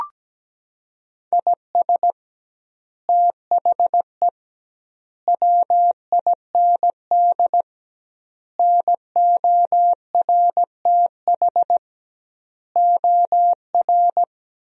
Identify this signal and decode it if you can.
Morse code, heard as a keyed tone, about 17 wpm, 700 Hz. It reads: IS THE WIND NORTH OR
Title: 2x Word-Spacing - Sentences from Top 400 Words - 17wpm